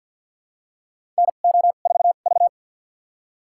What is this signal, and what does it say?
Morse code, heard as a keyed tone, about 49 wpm, spaced as for 35 wpm, 700 Hz. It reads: NO4V
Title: USA Call Signs - 2x Character Spacing - 50wpm